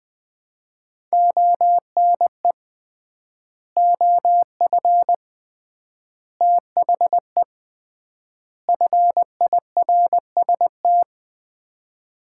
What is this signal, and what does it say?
Morse code, heard as a keyed tone, about 20 wpm, 700 Hz. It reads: ONE OF THE FIRST
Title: 3x Word-Spacing - Sets of 4 Words - 20wpm